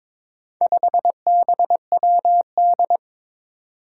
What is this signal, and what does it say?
Morse code, heard as a keyed tone, about 21 wpm, 700 Hz. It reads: 5BWD
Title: Four Challenging Letter-Number Combinations 22wpm